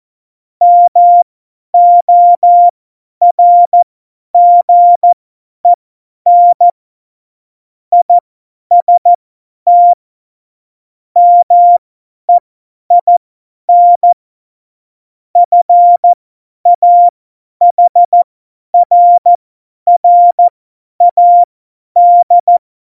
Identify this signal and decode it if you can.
Morse code, heard as a keyed tone, about 14 wpm, spaced as for 10 wpm, 700 Hz. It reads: MORGEN IST MEIN FAHRRAD
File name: Morsekurs / Witze / Witz 27. November 2018, 50 F70